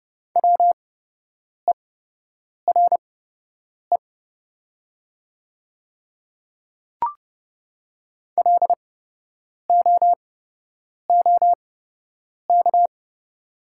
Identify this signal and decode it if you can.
Morse code, heard as a keyed tone, about 30 wpm, spaced as for 8 wpm, 700 Hz. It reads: WERE LOOK
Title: 8x Character Spacing - Top 100 Words - 30wpm